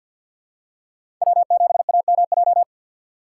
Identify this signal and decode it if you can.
Morse code, heard as a keyed tone, about 50 wpm, 700 Hz. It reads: W7AMJ
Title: USA Call Signs - 5 Characters - 3x Character Spacing - 50wpm